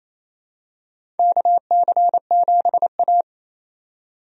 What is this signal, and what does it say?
Morse code, heard as a keyed tone, about 28 wpm, 700 Hz. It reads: KC7A
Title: USA Call Signs - 4 Characters - 28wpm